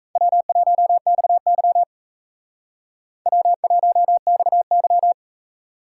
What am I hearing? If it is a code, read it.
Morse code, heard as a keyed tone, about 35 wpm, 700 Hz. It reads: W1XY W1XY